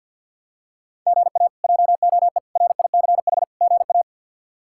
Morse code, heard as a keyed tone, about 47 wpm, 700 Hz. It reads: GA JOE RICH GA
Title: K1USN SST QSO Practice - 50wpm